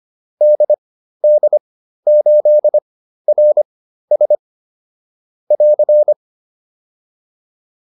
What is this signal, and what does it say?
Morse code, heard as a keyed tone, about 25 wpm, spaced as for 13 wpm, 600 Hz. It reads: DD8RS <AR>